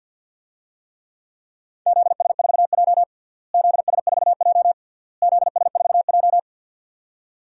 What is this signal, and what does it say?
Morse code, heard as a keyed tone, about 50 wpm, 700 Hz. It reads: ZS4J ZS4J ZS4J
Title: DXCC-100 Call Signs - 4 Characters - Repeated 3x - 50wpm